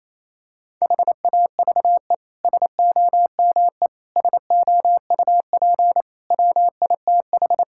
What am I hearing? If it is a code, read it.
Morse code, heard as a keyed tone, about 28 wpm, 700 Hz. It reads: HAVE SOME SOUP WITH
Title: Sentences from Top 1000 Words - Review Part 6 - 28wpm